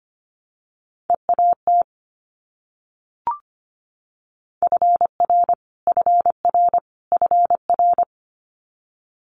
Morse code, heard as a keyed tone, about 25 wpm, 700 Hz. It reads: EAT FR FR FR
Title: Multi-letter Phonemes Repeated 3x 25wpm